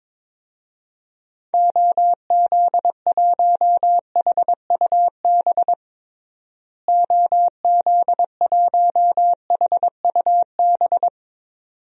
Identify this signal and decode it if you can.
Morse code, heard as a keyed tone, about 23 wpm, 700 Hz. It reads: OZ1HUB OZ1HUB
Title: DXCC-100 Call Signs - 6 Characters - Repeated 2x - 22wpm